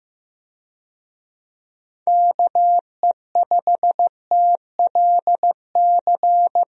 Morse code, heard as a keyed tone, about 15 wpm, 700 Hz. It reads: KE5TLC
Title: USA Call Signs - 6 Characters - Repeated 3x - 15wpm